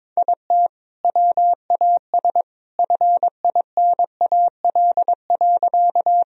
Morse code, heard as a keyed tone, about 22 wpm, 700 Hz. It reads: IT WAS FINAL.
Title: Sentences from Top 1000 Words - Review Part 2 - 22wpm